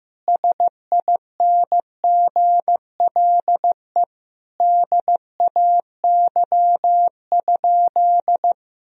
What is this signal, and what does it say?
Morse code, heard as a keyed tone, about 15 wpm, 700 Hz. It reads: SINGLE DAY?
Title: Sentences from Top 1000 Words - Review Part 4 - 15wpm